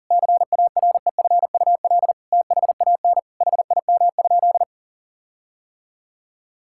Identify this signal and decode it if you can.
Morse code, heard as a keyed tone, about 40 wpm, 700 Hz. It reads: CAREFUL THAN HIM?
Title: Sentences from Top 1600 Words 40wpm